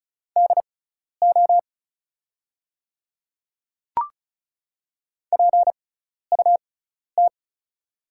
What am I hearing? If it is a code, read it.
Morse code, heard as a keyed tone, about 34 wpm, spaced as for 12 wpm, 690 Hz. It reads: DO PUT
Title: 6x Character Spacing - Top 200 Words - 35wpm